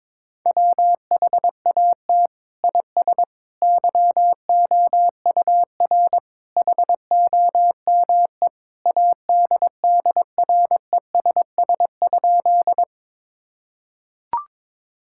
Morse code, heard as a keyed tone, about 22 wpm, 700 Hz. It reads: WHAT IS YOUR HOME ADDRESS?
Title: Sentences from Top 1000 Words - Review Part 6 - 22wpm